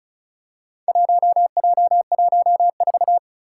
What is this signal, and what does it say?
Morse code, heard as a keyed tone, about 36 wpm, 700 Hz. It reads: 1J14
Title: Four Challenging Letter-Number Combinations 35wpm